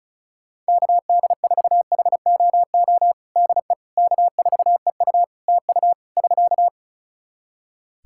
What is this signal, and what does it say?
Morse code, heard as a keyed tone, about 34 wpm, 700 Hz. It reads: KD4HOO DE K4EU TU <SK>
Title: Practice QSO - Exchange 4 - Ending - 35wpm